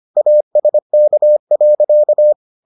Morse code, heard as a keyed tone, about 25 wpm, 600 Hz. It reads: ASK.